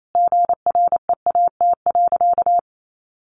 Morse code, heard as a keyed tone, about 28 wpm, 700 Hz. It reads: GREAT.
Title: Sentences from Top 800 Words 28wpm